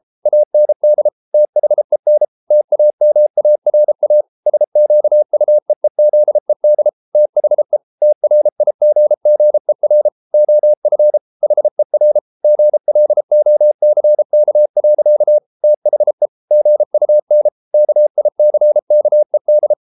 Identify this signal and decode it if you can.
Morse code, heard as a keyed tone, about 33 wpm, 600 Hz. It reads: AND THEN TAMARA SQUEEZED THE TRIGGER OF HER GLOCK. THE GUN KICKED